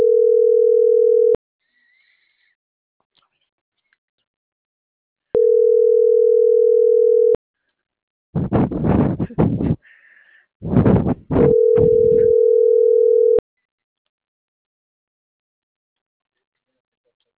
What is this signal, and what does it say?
Ringback tone (North America), 440 + 480 Hz.